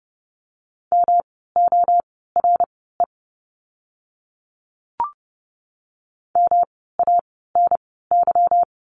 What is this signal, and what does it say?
Morse code, heard as a keyed tone, about 30 wpm, spaced as for 17 wpm, 700 Hz. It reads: MORE MANY
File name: course / 30:10wpm / 237 Lesson 38f - Letter q - Words 30wpm